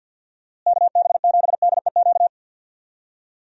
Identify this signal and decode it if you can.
Morse code, heard as a keyed tone, about 51 wpm, 700 Hz. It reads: KB7DEQ
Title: USA Call Signs - 6 Characters - Repeated 2x - 50wpm